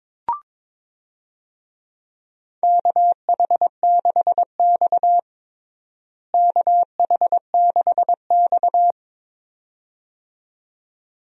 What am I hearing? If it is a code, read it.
Morse code, heard as a keyed tone, about 22 wpm, 700 Hz. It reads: KH6X KH6X
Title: DXCC-100 Call Signs - 4 Characters - Repeated 2x - 22wpm